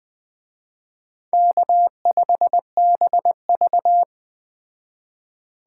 Morse code, heard as a keyed tone, about 20 wpm, 700 Hz. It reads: K5BV
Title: Speed Racing - USA Call Signs 17wpm